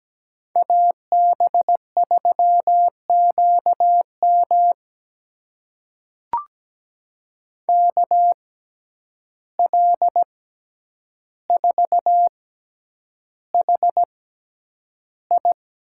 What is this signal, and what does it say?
Morse code, heard as a keyed tone, about 17 wpm, spaced as for 9 wpm, 700 Hz. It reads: AB3QM KL4HI
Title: USA Call Signs - 6x Character Spacing - 17wpm